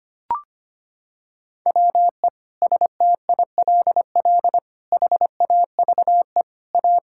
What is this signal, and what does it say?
Morse code, heard as a keyed tone, about 25 wpm, 700 Hz. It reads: WE STILL HAVE A
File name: Sets of 9 Words - 25wpm